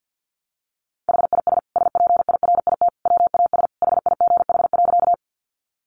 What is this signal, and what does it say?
Morse code, heard as a keyed tone, about 50 wpm, 700 Hz. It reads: HIS SPIRIT WAS HIGH.
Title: Sentences from Top 1400 Words 50wpm